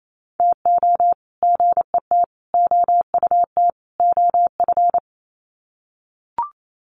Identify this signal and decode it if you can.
Morse code, heard as a keyed tone, about 28 wpm, 700 Hz. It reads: TO GET OUT OF
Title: Sets of 4 Words 28wpm